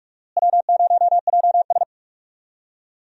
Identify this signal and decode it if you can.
Morse code, heard as a keyed tone, about 46 wpm, 700 Hz. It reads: W0JS
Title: USA Call Signs - 4 Characters - Repeated 3x - 45wpm